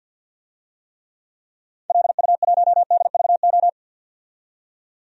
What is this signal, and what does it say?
Morse code, heard as a keyed tone, about 47 wpm, 700 Hz. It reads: LU1DVO
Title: DXCC-100 Call Signs - 6 Characters - 50wpm